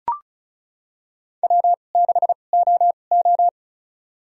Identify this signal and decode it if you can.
Morse code, heard as a keyed tone, about 36 wpm, spaced as for 26 wpm, 690 Hz. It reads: W6OO
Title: USA Call Signs - 4 Characters - 2x Character Spacing - 35wpm